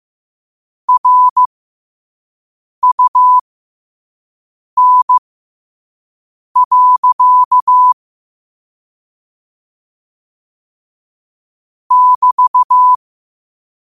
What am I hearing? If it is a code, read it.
Morse code, heard as a keyed tone, about 15 wpm, spaced as for 5 wpm, 1000 Hz. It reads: RUN. <BT>
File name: qotdfiles / 5wpm